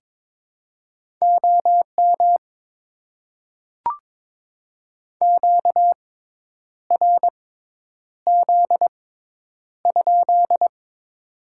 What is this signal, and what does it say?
Morse code, heard as a keyed tone, about 22 wpm, spaced as for 9 wpm, 700 Hz. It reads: OM QRZ?
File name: Top 90 QSO Elements - 6x Character Spacing - 22wpm